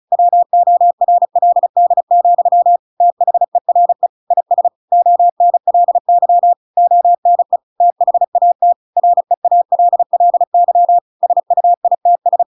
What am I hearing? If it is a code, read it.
Morse code, heard as a keyed tone, about 34 wpm, 700 Hz. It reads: WORLD, THERE IS ONLY ONE THAT REALLY SUITS